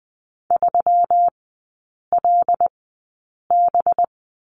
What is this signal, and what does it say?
Morse code, heard as a keyed tone, about 20 wpm, spaced as for 8 wpm, 700 Hz. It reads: 3LB
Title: 184 Lesson 30e - Letter b - Three Characters 20wpm